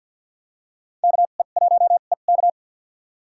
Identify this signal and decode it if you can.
Morse code, heard as a keyed tone, about 48 wpm, spaced as for 35 wpm, 700 Hz. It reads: KE1EK
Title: USA Call Signs - 5 Characters - 2x Character Spacing - 50wpm